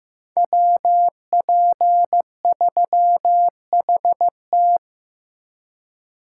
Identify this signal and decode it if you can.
Morse code, heard as a keyed tone, about 18 wpm, 700 Hz. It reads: WP3HT
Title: Speed Racing - USA 2x2 Call Signs 15wpm